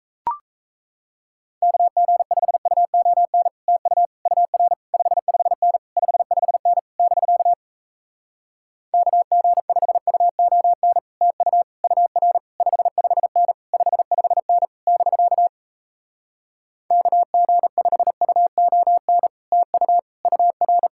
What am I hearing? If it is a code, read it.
Morse code, heard as a keyed tone, about 38 wpm, 700 Hz. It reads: KG5UON TU UR 55N 55N <BK> KG5UON TU UR 55N 55N <BK> KG5UON TU UR